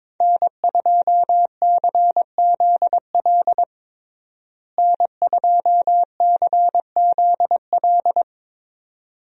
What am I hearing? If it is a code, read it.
Morse code, heard as a keyed tone, about 22 wpm, 700 Hz. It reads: N2CZL N2CZL
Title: USA Call Signs - 5 Characters - Repeated 2x - 22wpm